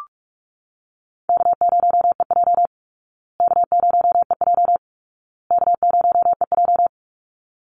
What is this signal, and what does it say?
Morse code, heard as a keyed tone, about 44 wpm, 700 Hz. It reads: K0EJ K0EJ K0EJ
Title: Easy Call Signs Repeated 3x - 45wpm